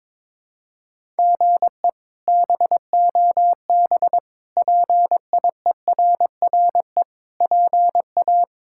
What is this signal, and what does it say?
Morse code, heard as a keyed tone, about 22 wpm, 700 Hz. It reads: GE BOB PIERRE PA
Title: K1USN SST QSO Practice - 22wpm